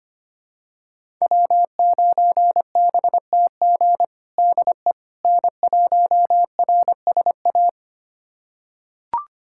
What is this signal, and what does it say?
Morse code, heard as a keyed tone, about 25 wpm, 700 Hz. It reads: W9BTG DE N1RSA
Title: USA Callsign DE Callsign - 5 Characters - 25wpm